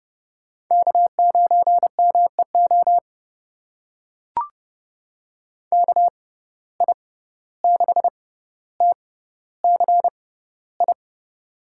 Morse code, heard as a keyed tone, about 30 wpm, spaced as for 15 wpm, 700 Hz. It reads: K9MEO KI6TCI
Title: USA Call Signs - 6x Character Spacing - 30wpm